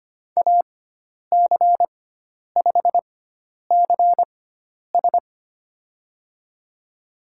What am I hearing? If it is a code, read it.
Morse code, heard as a keyed tone, about 25 wpm, spaced as for 10 wpm, 700 Hz. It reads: AC5CS
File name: Lesson 16g - Letter c - Call Signs 25wpm